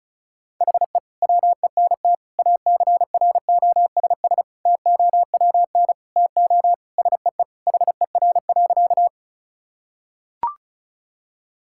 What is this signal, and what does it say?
Morse code, heard as a keyed tone, about 34 wpm, 700 Hz. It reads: HE WENT ACROSS TOWN TO SEE HER.
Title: Sentences from Top 1000 Words - Review Part 5 - 35wpm